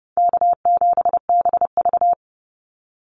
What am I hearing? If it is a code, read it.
Morse code, heard as a keyed tone, about 30 wpm, 700 Hz. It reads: K7BV
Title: Easy Call Signs - 30wpm